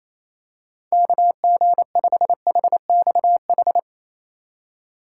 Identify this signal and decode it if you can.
Morse code, heard as a keyed tone, about 28 wpm, 700 Hz. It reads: KG5HXH
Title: USA Call Signs - 6 Characters - 4x Character Spacing - 28wpm